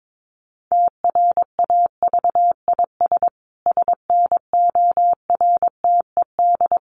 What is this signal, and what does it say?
Morse code, heard as a keyed tone, about 22 wpm, 700 Hz. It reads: TRAVIS SNORTED
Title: A Sound Of Thunder 22wpm